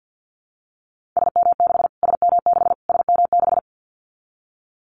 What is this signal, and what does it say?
Morse code, heard as a keyed tone, about 50 wpm, 700 Hz. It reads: SM6 SM6 SM6
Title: Callsign Prefixes for Top 25 DXCC - Repeated 3x - 50wpm